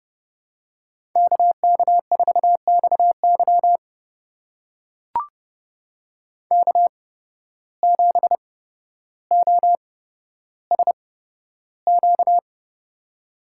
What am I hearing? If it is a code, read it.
Morse code, heard as a keyed tone, about 31 wpm, spaced as for 13 wpm, 700 Hz. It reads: KK4XY K7OSQ
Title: USA Call Signs - 5 Characters - 8x Character Spacing - 30wpm